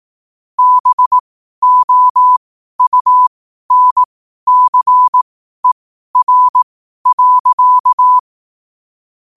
Morse code, heard as a keyed tone, about 18 wpm, spaced as for 12 wpm, 1000 Hz. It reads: BOUNCER.
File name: qotdfiles / 12wpm